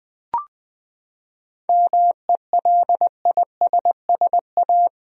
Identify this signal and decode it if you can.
Morse code, heard as a keyed tone, about 20 wpm, 700 Hz. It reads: MELISSA
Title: USA Common Names - 20wpm